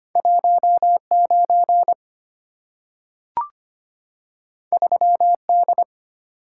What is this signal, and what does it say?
Morse code, heard as a keyed tone, about 25 wpm, 700 Hz. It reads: 19 3D
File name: Two Letter-Number Combinations - 25wpm